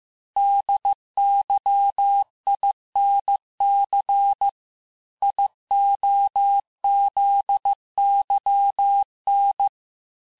Morse code, heard as a keyed tone, about 15 wpm, 800 Hz. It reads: DYINC IOZYN
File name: cours-du-lundi files / cours-1585-texte-chiffre-vitesse-14mpm